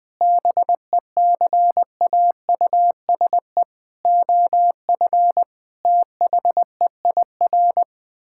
Morse code, heard as a keyed tone, about 20 wpm, 700 Hz. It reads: BECAUSE OF THEIR
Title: Sentences with 9-Word Phrases - Spoken Beginning - 20wpm